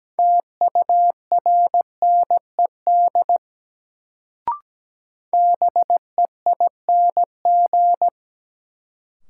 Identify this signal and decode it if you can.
Morse code, heard as a keyed tone, about 17 wpm, 700 Hz. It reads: TURNED BEING
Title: Sets of 4 Words - New Word Drill - 17wpm